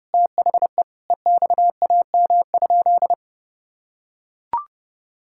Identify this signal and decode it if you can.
Morse code, heard as a keyed tone, about 30 wpm, 700 Hz. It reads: THE EXAM?
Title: Sentences from Top 1000 Words - Review Part 5 - 30wpm